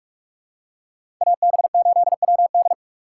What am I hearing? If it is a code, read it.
Morse code, heard as a keyed tone, about 44 wpm, 690 Hz. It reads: AB8WD
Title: USA Call Signs - 5 Characters - 6x Character Spacing - 45wpm